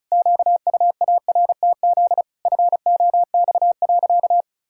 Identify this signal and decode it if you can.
Morse code, heard as a keyed tone, about 34 wpm, 700 Hz. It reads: QUARTZ FOX.
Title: Pangram Challenge v2 - 35wpm